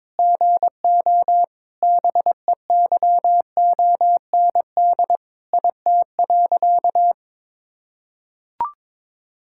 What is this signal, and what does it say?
Morse code, heard as a keyed tone, about 22 wpm, 700 Hz. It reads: GO BEYOND IT.
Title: Sentences from Top 1000 Words - Review Part 6 - 22wpm